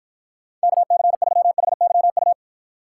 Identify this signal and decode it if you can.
Morse code, heard as a keyed tone, about 52 wpm, 700 Hz. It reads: KC3HYU